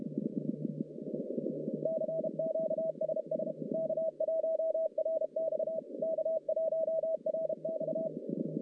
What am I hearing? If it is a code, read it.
Morse code, heard as a keyed tone, about 32 wpm, 640 Hz, with noise. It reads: CQ SS K1RX K1RX
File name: contests / CW-SS-k5tr- -1200